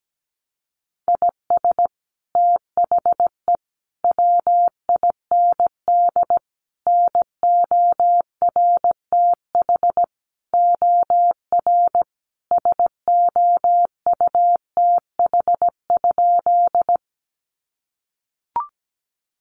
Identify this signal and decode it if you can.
Morse code, heard as a keyed tone, about 17 wpm, 700 Hz. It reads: IS THE WIND NORTH OR SOUTH?
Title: Sentences from Top 400 Words 17wpm